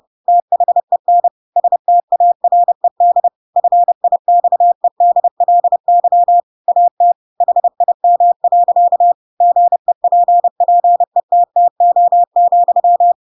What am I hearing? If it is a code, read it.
Morse code, heard as a keyed tone, about 30 wpm, 700 Hz. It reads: THEN STARED FIXEDLY AT HIM. GEPPETTO,
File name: Pinocchio / Pinocchio ch3 30 30 ik1hge